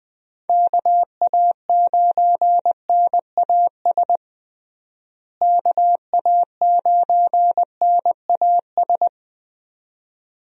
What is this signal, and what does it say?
Morse code, heard as a keyed tone, about 21 wpm, 700 Hz. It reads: KA9NAS KA9NAS
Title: USA Call Signs - 6 Characters - Repeated 2x - 20wpm